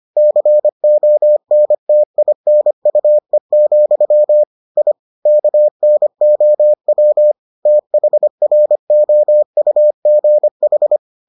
Morse code, heard as a keyed tone, about 25 wpm, 600 Hz. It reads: CONTINUE, I KNOW THROUGH